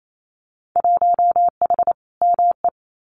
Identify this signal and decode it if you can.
Morse code, heard as a keyed tone, about 28 wpm, 700 Hz. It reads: 1H ME
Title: Winter Field Day Exchanges 28wpm